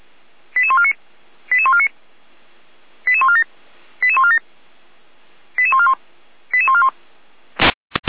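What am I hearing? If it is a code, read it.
Five-tone selective call (ZVEI), heard as quick runs of five tones, tones of about 70 ms. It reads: 88149 88149 88147 88147 88141 88141
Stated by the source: ZVEI five-tone selective call received by SDR.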